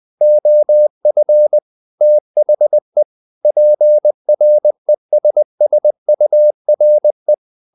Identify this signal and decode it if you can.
Morse code, heard as a keyed tone, about 20 wpm, 600 Hz. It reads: OF THE PRESSURE
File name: the old man and the sea part 2 20wpm